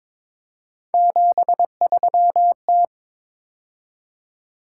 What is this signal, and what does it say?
Morse code, heard as a keyed tone, about 26 wpm, 700 Hz. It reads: 73T
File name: Speed-Racing - Three Letter-Number Combinations - 22wpm